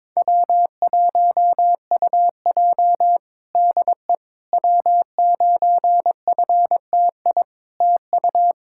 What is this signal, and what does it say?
Morse code, heard as a keyed tone, about 21 wpm, 700 Hz. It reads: W1UJ DE W9FTI TU
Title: Practice QSO - Exchange 4 - Ending - 22wpm